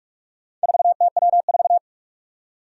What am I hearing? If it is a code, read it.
Morse code, heard as a keyed tone, about 44 wpm, 700 Hz. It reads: 4TW4
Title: Four Challenging Letter-Number Combinations 45wpm